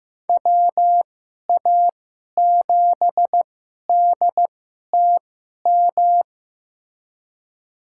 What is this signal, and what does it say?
Morse code, heard as a keyed tone, about 15 wpm, spaced as for 11 wpm, 700 Hz. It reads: WA7DTM
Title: USA Call Signs - 6 Characters - 2x Character Spacing - 15wpm